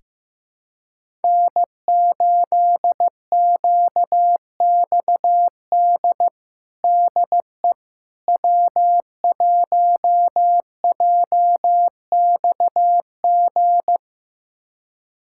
Morse code, heard as a keyed tone, about 15 wpm, 700 Hz. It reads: N8QXD DE W1JXG
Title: USA Callsign DE Callsign 15wpm